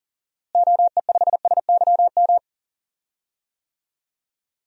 Morse code, heard as a keyed tone, about 40 wpm, 700 Hz. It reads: OE5SYM
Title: DXCC-100 Call Signs - 6 Characters - Repeated 2x - 40wpm